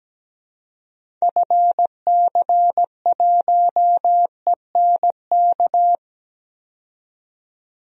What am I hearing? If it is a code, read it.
Morse code, heard as a keyed tone, about 17 wpm, 700 Hz. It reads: FC1ENK
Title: DXCC-100 Call Signs - 6 Characters - 17wpm